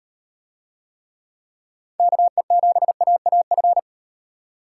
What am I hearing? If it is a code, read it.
Morse code, heard as a keyed tone, about 38 wpm, 700 Hz. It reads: KE7AAF